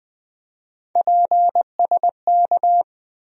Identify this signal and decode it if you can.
Morse code, heard as a keyed tone, about 20 wpm, 700 Hz. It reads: PSK
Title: USA Call Sign Postfixes - 3 Letters 20wpm